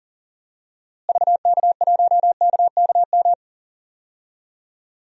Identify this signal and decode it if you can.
Morse code, heard as a keyed tone, about 39 wpm, 700 Hz. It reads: VK1KKM
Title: DXCC-100 Call Signs - 6 Characters - Repeated 3x - 40wpm